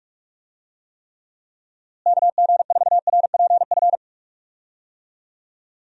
Morse code, heard as a keyed tone, about 45 wpm, 700 Hz. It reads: KG4RPF
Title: USA Call Signs - 6 Characters - Repeated 3x - 45wpm